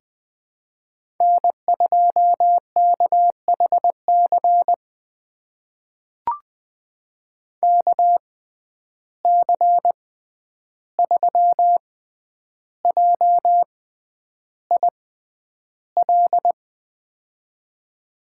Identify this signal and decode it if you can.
Morse code, heard as a keyed tone, about 20 wpm, spaced as for 10 wpm, 700 Hz. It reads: N2KHC KC3JIL
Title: USA Call Signs - 6x Character Spacing - 20wpm